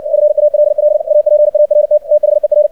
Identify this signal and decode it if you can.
Morse code, heard as a keyed tone, about 60 wpm, 600 Hz, with radio noise. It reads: FALL APART THER
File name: archived / CRSnet